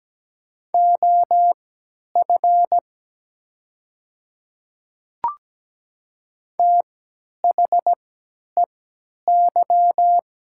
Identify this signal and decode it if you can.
Morse code, heard as a keyed tone, about 17 wpm, spaced as for 10 wpm, 690 Hz. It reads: OF THEY
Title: 3x Character Spacing - Top 100 Words - 17wpm